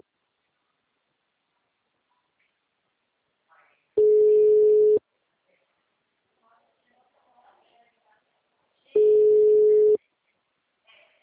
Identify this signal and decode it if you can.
Ringback tone (Europe, 425 Hz), with speech.